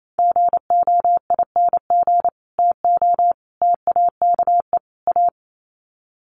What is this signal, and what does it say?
Morse code, heard as a keyed tone, about 28 wpm, 700 Hz. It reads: GOING TO TAKE A
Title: Sets of 4 Words 28wpm